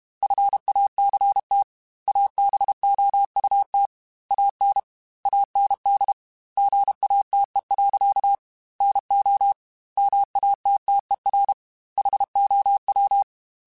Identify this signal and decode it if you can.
Morse code, heard as a keyed tone, about 32 wpm, 800 Hz. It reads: FACT ABOUT AN AND GATE. NO MATTER HOW
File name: Technical Subjects / Technical Subjects-mp3s / AND Gate @30